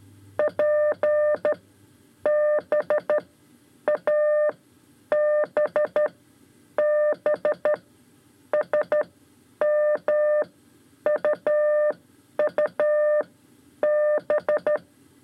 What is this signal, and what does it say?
Morse code, heard as a keyed tone, about 12 wpm, spaced as for 9 wpm, 600 Hz. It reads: PBABB SMUUB
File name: 5 Lezione CW 2